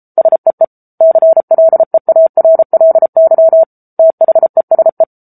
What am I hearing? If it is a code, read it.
Morse code, heard as a keyed tone, about 34 wpm, 650 Hz. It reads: SEE CLEARLY THESE
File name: Book Two / 07-Chapter 7 - 33 WPM